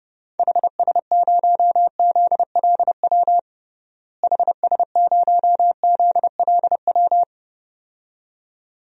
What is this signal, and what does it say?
Morse code, heard as a keyed tone, about 31 wpm, 700 Hz. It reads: HS0ZLW HS0ZLW
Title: DXCC-100 Call Signs - 6 Characters - Repeated 2x - 30wpm